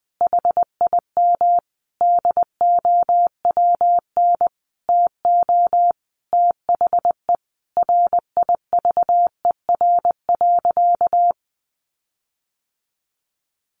Morse code, heard as a keyed tone, about 20 wpm, 700 Hz. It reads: HIM DOWN TO THE RIVER.
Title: Sentences from Top 1500 Words 20wpm